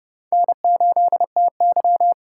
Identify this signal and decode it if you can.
Morse code, heard as a keyed tone, about 40 wpm, 700 Hz. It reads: N8TY